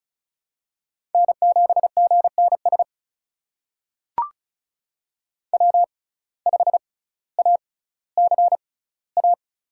Morse code, heard as a keyed tone, about 36 wpm, spaced as for 18 wpm, 700 Hz. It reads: N7GNS W5ACA
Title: USA Call Signs - 6x Character Spacing - 35wpm